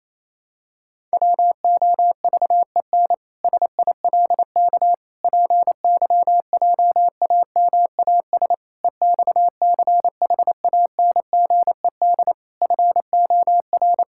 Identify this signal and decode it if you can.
Morse code, heard as a keyed tone, about 28 wpm, 700 Hz. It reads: WOVEN SILK PYJAMAS EXCHANGED FOR
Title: Pangram Challenge v2 - 28wpm